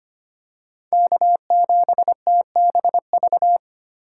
Morse code, heard as a keyed tone, about 25 wpm, 700 Hz. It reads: K7TBV
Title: USA Call Signs - 5 Characters - 3x Character Spacing - 25wpm